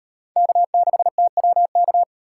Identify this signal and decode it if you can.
Morse code, heard as a keyed tone, about 38 wpm, 700 Hz. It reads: K6TWK